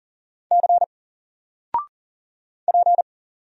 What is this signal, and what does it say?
Morse code, heard as a keyed tone, about 40 wpm, 700 Hz. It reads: C P
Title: Single Letter-Number 40wpm